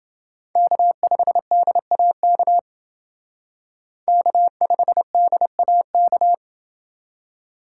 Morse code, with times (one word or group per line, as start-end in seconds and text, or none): 0.6-2.6 K5DAK
4.1-6.3 K5DAK